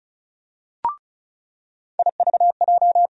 Morse code, heard as a keyed tone, about 36 wpm, 700 Hz. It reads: IVJ
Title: USA Call Sign Postfixes - 3 Letters 35wpm